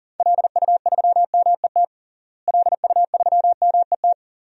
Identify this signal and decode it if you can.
Morse code, heard as a keyed tone, about 40 wpm, 700 Hz. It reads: LU3MET LU3MET
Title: DXCC-100 Call Signs - 6 Characters - Repeated 3x - 40wpm